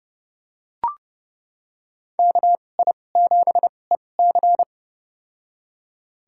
Morse code, heard as a keyed tone, about 30 wpm, spaced as for 22 wpm, 700 Hz. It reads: KI7EC
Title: USA Call Signs - 2x Character Spacing - 30wpm